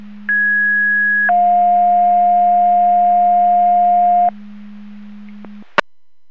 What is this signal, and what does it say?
Two-tone page, heard as a one-second tone, then a three-second tone, about 1650 then 710 Hz. It is heard from 0.3 to 4.3 s.
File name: Portage 154.385 Paging